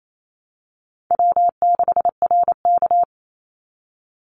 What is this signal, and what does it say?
Morse code, heard as a keyed tone, about 28 wpm, 700 Hz. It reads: W6RK
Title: Call Signs Repeated 3x - 28wpm